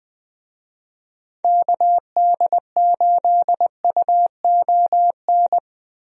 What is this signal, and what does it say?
Morse code, heard as a keyed tone, about 20 wpm, 700 Hz. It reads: KD8UON
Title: USA Call Signs - 6 Characters - Repeated 2x - 20wpm